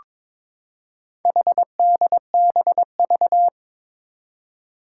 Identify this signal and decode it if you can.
Morse code, heard as a keyed tone, about 22 wpm, 700 Hz. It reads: HDBV
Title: Four Challenging Letter-Number Combinations 22wpm